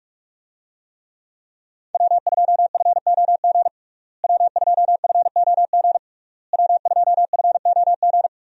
Morse code, heard as a keyed tone, about 43 wpm, 700 Hz. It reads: W2FOG W2FOG W2FOG
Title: USA Call Signs - 5 Characters - Repeated 3x - 45wpm